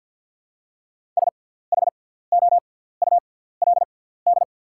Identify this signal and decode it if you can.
Morse code, heard as a keyed tone, about 48 wpm, spaced as for 17 wpm, 700 Hz. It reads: SHOULD
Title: 6x Character Spacing - Top 300 Words - 50wpm